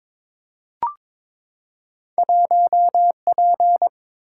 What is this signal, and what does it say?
Morse code, heard as a keyed tone, about 22 wpm, 700 Hz. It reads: 1 P
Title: Single-Letter Single-Number 22wpm